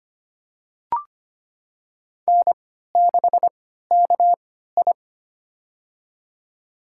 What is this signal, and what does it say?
Morse code, heard as a keyed tone, about 25 wpm, spaced as for 14 wpm, 700 Hz. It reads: N6KI
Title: USA Call Signs - 4 Characters - 3x Character Spacing - 25wpm